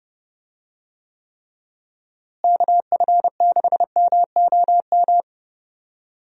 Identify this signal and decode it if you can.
Morse code, heard as a keyed tone, about 31 wpm, 700 Hz. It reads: KF6MOM
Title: USA Call Signs - 6 Characters - Repeated 3x - 30wpm